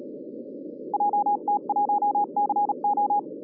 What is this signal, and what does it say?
Morse code, heard as a keyed tone, about 37 wpm, 820 Hz, with noise. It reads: JT1CO
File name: 2011 bcc-challenge samstag tape